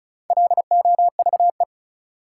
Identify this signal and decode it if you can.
Morse code, heard as a keyed tone, about 36 wpm, 700 Hz. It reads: LOVE
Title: Sentences from Top 400 Words - New Word Drill - 35wpm